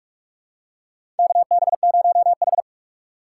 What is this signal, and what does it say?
Morse code, heard as a keyed tone, about 46 wpm, 700 Hz. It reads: KB0H
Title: USA Call Signs - 4 Characters - 8x Character Spacing - 45wpm